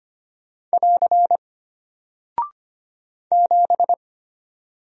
Morse code, heard as a keyed tone, about 25 wpm, 700 Hz. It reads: <AR> 7
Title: Single Character (Letter:Number:Punctuation:Prosign) Alt 25wpm